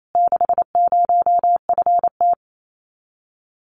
Morse code, heard as a keyed tone, about 28 wpm, 700 Hz. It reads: 60FT
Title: Common QSO Phrases Encore 28wpm